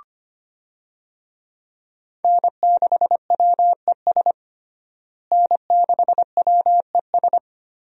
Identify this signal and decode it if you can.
Morse code, heard as a keyed tone, about 25 wpm, 700 Hz. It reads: N6WES N6WES
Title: USA Call Signs - 5 Characters - Repeated 3x - 25wpm